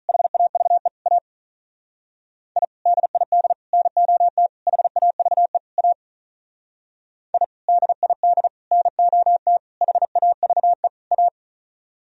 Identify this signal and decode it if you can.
Morse code, heard as a keyed tone, about 44 wpm, 700 Hz. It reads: HAVE A I DID NOT HAVE A I DID NOT HAVE A